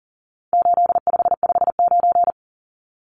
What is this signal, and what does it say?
Morse code, heard as a keyed tone, about 40 wpm, 700 Hz. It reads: 8559
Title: Numbers - 4-Digit 40wpm